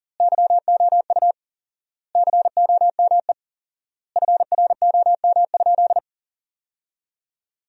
Morse code, heard as a keyed tone, about 40 wpm, 700 Hz. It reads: YOU COME FROM?
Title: 4x Word-Spacing - Sentences from Top 200 Words - 40wpm